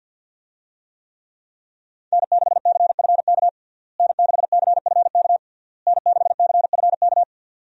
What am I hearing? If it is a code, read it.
Morse code, heard as a keyed tone, about 50 wpm, 700 Hz. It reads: N6CFK N6CFK N6CFK
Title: USA Call Signs - 5 Characters - Repeated 3x - 50wpm